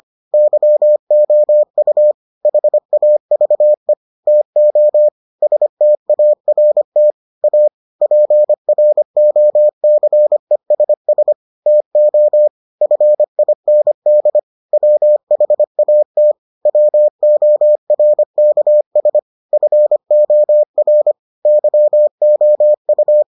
Morse code, heard as a keyed tone, about 25 wpm, 600 Hz. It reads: YOU HAVE TO START A PROCESS TO FIND WHAT WORKS FOR YOU